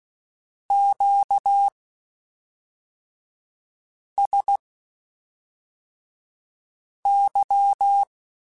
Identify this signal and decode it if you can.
Morse code, heard as a keyed tone, about 16 wpm, spaced as for 3 wpm, 800 Hz. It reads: QSY
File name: (spaced) / Q-Calls Spaced-mp3s / Q Calls sp Once @5